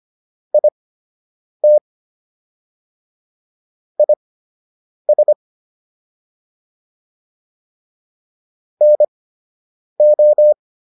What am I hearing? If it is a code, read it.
Morse code, heard as a keyed tone, about 25 wpm, spaced as for 8 wpm, 600 Hz. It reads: IT IS NO